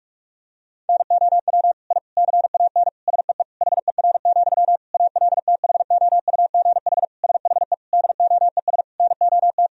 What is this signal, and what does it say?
Morse code, heard as a keyed tone, about 46 wpm, 700 Hz. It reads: NOW I CAN SEE HER, ALTHOUGH SHE DOES NOT
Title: Sentences from Top 1000 Words - Review Part 3 - 45wpm